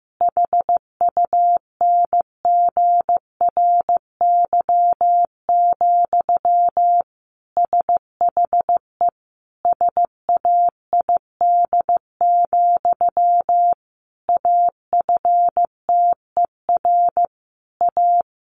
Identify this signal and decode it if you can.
Morse code, heard as a keyed tone, about 15 wpm, 700 Hz. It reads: HUNGRY, SHE SAID, AFTER A
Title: Sentences from Top 1400 Words 15wpm